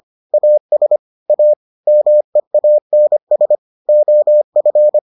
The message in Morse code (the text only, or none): AS A MEANS OF